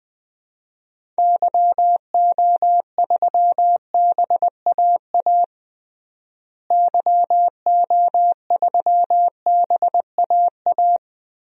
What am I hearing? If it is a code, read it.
Morse code, heard as a keyed tone, about 20 wpm, 700 Hz. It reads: YO3BAA YO3BAA
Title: DXCC-100 Call Signs - 6 Characters - Repeated 3x - 20wpm